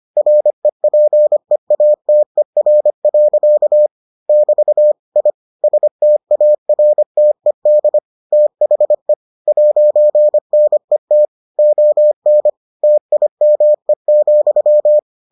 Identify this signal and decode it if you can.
Morse code, heard as a keyed tone, about 25 wpm, 600 Hz. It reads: REPEATER. <BT> I STARTED THE 'NET ON TIME,